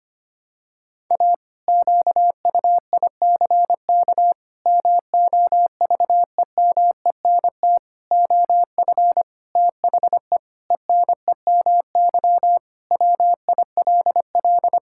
Morse code, heard as a keyed tone, about 25 wpm, 700 Hz. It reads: A QUICK MOVEMENT OF THE ENEMY WILL
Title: Pangram Challenge v2 - 25wpm